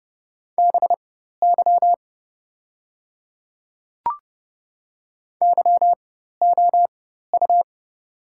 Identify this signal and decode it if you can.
Morse code, heard as a keyed tone, about 30 wpm, spaced as for 14 wpm, 700 Hz. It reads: BY YOU
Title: 4x Character Spacing - Top 200 Words - 30wpm